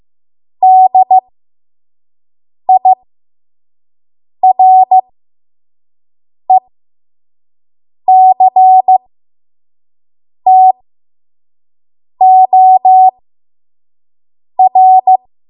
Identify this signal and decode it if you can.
Morse code, heard as a keyed tone, about 15 wpm, spaced as for 5 wpm, 750 Hz. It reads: DIRECTOR
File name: Morse / 05WPM